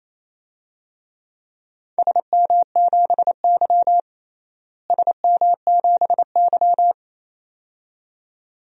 Morse code, heard as a keyed tone, about 28 wpm, 700 Hz. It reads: SM7Y SM7Y
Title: DXCC-100 Call Signs - 4 Characters - Repeated 2x - 28wpm